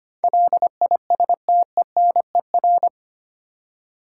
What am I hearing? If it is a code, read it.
Morse code, heard as a keyed tone, about 25 wpm, 700 Hz. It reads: LISTENER
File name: Eight Letter Words - 25wpm